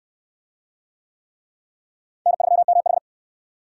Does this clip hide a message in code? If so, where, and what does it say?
Yes: Morse code from 2.3 to 3.0 s, heard as a keyed tone, about 68 wpm, 700 Hz. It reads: N3MH